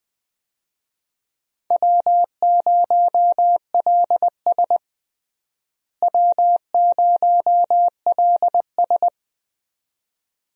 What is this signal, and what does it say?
Morse code, heard as a keyed tone, about 20 wpm, 700 Hz. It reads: W0LS W0LS
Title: USA Call Signs - 4 Characters - Repeated 2x - 20wpm